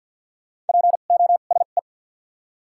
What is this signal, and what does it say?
Morse code, heard as a keyed tone, about 48 wpm, spaced as for 35 wpm, 700 Hz. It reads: POSE
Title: 2x Character Spacing - Top 500 Words - 50wpm